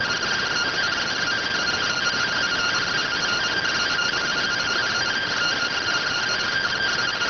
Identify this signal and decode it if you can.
Two-tone teleprinter signal, about 170 Hz shift.